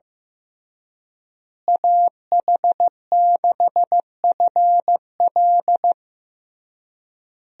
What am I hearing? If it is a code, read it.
Morse code, heard as a keyed tone, about 16 wpm, 700 Hz. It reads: AH6FL
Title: USA Call Signs - 5 Characters - Repeated 3x - 15wpm